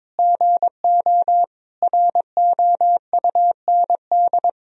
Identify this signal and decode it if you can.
Morse code, heard as a keyed tone, about 23 wpm, 700 Hz. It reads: GO ROUND
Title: Sentences from Top 1000 Words - Review Part 1 - 22wpm